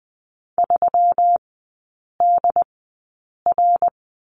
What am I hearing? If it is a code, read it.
Morse code, heard as a keyed tone, about 20 wpm, spaced as for 8 wpm, 700 Hz. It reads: 3DR
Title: 121 Lesson 20e - Number 3 - Three Characters 20wpm